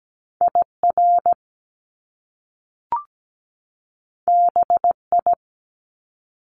Morse code, heard as a keyed tone, about 17 wpm, 700 Hz. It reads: IR BI
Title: Top Two Letter Combinations in Words 17wpm